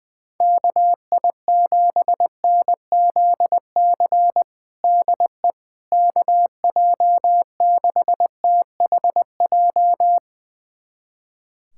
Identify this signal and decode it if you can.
Morse code, heard as a keyed tone, about 20 wpm, 700 Hz. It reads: KI7NZC DE KJ6THJ
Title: USA Callsign DE Callsign - 6 Characters - 20wpm